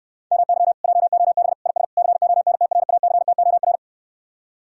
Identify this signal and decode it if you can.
Morse code, heard as a keyed tone, about 69 wpm, 700 Hz. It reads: MY JOB IS CONTRACTOR